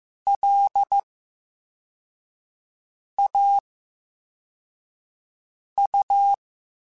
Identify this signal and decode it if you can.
Morse code, heard as a keyed tone, about 15 wpm, 800 Hz. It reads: L A U
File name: Special INGs-mp3s / Special INGs-mp3s / VerySlowToFaster Ing @13